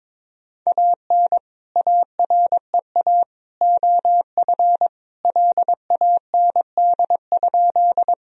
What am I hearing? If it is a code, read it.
Morse code, heard as a keyed tone, about 22 wpm, 700 Hz. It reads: AN AREA OF LAND?
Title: Sentences from Top 1000 Words - Review Part 2 - 22wpm